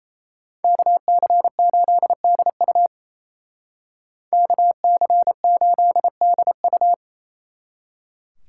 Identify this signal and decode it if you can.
Morse code, heard as a keyed tone, about 33 wpm, 700 Hz. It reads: KC8DU KC8DU
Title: Speed Racing - USA 2x2 Call Signs 28wpm